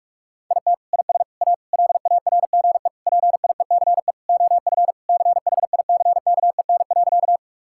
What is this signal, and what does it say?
Morse code, heard as a keyed tone, about 41 wpm, 700 Hz. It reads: IT IS A LARGE PIECE OF CHICKEN.
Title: Sentences from Top 1000 Words - Review Part 5 - 45wpm